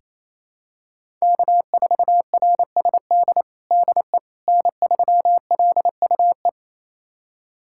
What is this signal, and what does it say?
Morse code, heard as a keyed tone, about 28 wpm, 700 Hz. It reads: K4RSD DE N3LUE
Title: USA Callsign DE Callsign - 5 Characters - 28wpm